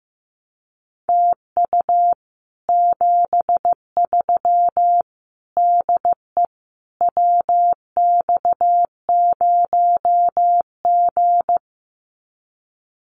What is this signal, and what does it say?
Morse code, heard as a keyed tone, about 15 wpm, 700 Hz. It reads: TU 73 DE WX0G
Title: Common QSO Phrases Encore 15wpm